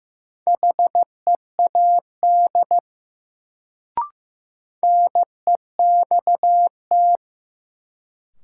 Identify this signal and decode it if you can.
Morse code, heard as a keyed tone, about 15 wpm, 700 Hz. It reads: HEAD NEXT
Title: Sentences from Top 300 Words - New Word Drill - 15wpm